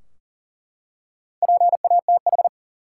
Morse code, heard as a keyed tone, about 40 wpm, 700 Hz. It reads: PATH
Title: Four Letter Words - 40wpm